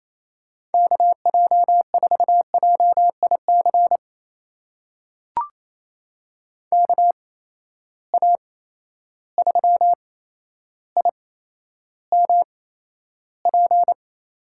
Morse code, heard as a keyed tone, about 28 wpm, spaced as for 12 wpm, 700 Hz. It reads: KJ4JIC KA3IMP
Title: USA Call Signs - 6 Characters - 8x Character Spacing - 28wpm